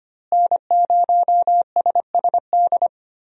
Morse code, heard as a keyed tone, about 25 wpm, 700 Hz. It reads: N0SSD
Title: USA Call Signs - 5 Characters - Repeated 3x - 25wpm